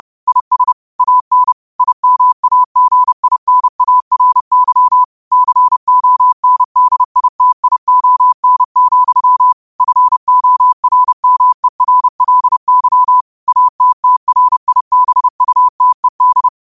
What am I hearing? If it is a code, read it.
Morse code, heard as a keyed tone, about 30 wpm, 1000 Hz. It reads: IS AN IMAGINARY CONDITION, FORMERLY ATTRIBUTED
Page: Quotes for Sun, 17 Aug 2025 in Morse Code at 30 words per minute.